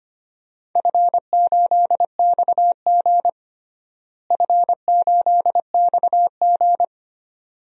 Morse code, heard as a keyed tone, about 25 wpm, 700 Hz. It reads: F8XG F8XG
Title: DXCC-100 Call Signs - 4 Characters - Repeated 2x - 25wpm